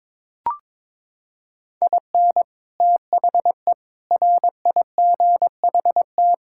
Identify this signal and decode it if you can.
Morse code, heard as a keyed tone, about 22 wpm, 700 Hz. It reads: IN THE RIGHT
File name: practice / 22wpm / Sets of 8 Words - 22wpm